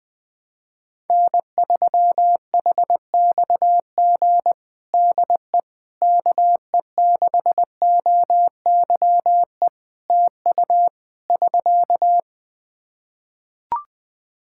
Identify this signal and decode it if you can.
Morse code, heard as a keyed tone, about 21 wpm, 700 Hz. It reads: N3HXG DE KE6OYE TU <SK>
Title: Practice QSO - Exchange 4 - Ending - 20wpm